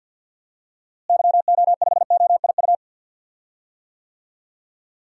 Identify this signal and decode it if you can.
Morse code, heard as a keyed tone, about 47 wpm, 700 Hz. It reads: YO5OIU
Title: DXCC-100 Call Signs - 6 Characters - Repeated 3x - 50wpm